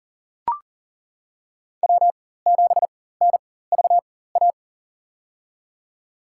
Morse code, heard as a keyed tone, about 40 wpm, spaced as for 19 wpm, 700 Hz. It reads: W7NVA
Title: USA Call Signs - 4x Character Spacing - 40wpm